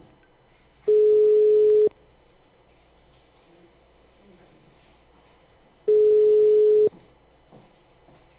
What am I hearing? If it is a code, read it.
Ringback tone (Europe, 425 Hz).
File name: out- -2001- - - .64826